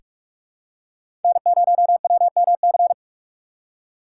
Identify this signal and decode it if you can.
Morse code, heard as a keyed tone, about 41 wpm, 700 Hz. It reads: N0WMC
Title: USA Call Signs - 5 Characters - 4x Character Spacing - 45wpm